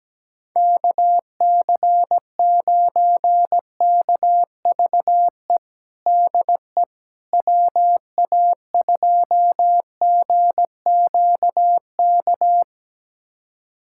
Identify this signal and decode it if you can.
Morse code, heard as a keyed tone, about 17 wpm, 700 Hz. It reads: KC9KVE DE WA2GQK
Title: USA Callsign DE Callsign - 6 Characters - 17wpm